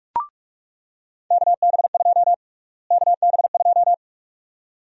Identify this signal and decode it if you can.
Morse code, heard as a keyed tone, about 46 wpm, 700 Hz. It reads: KB2 KB2
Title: USA Call Sign Prefixes - Repeated 2x - 45wpm